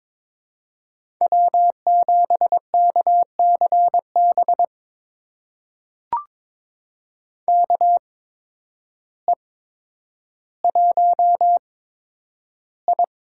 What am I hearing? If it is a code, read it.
Morse code, heard as a keyed tone, about 22 wpm, spaced as for 10 wpm, 700 Hz. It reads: W7KCB KE1I
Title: USA Call Signs - 5 Characters - 8x Character Spacing - 22wpm